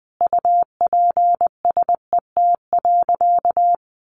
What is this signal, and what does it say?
Morse code, heard as a keyed tone, about 20 wpm, 700 Hz. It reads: UPSET.
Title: Sentences from Top 1200 Words 20wpm